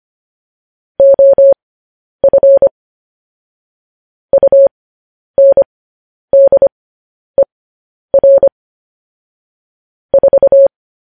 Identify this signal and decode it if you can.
Morse code, heard as a keyed tone, about 25 wpm, spaced as for 10 wpm, 550 Hz. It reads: OF UNDER 4
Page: news headlines in morse code at 10 wpm